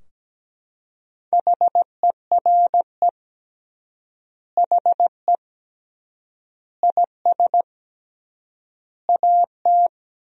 Morse code, heard as a keyed tone, about 17 wpm, 700 Hz. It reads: HERE HE IS AT
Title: 3x Word-Spacing - Sentences from Top 300 Words - 17wpm